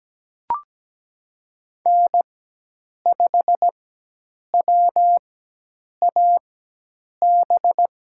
Morse code, heard as a keyed tone, about 17 wpm, spaced as for 8 wpm, 700 Hz. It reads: N5WAB
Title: USA Call Signs - 5 Characters - 4x Character Spacing - 17wpm